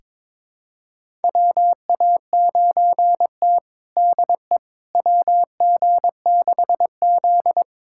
Morse code, heard as a keyed tone, about 22 wpm, 700 Hz. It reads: WA9T DE WG6Z
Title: USA Callsign DE Callsign - 4 Characters - 22wpm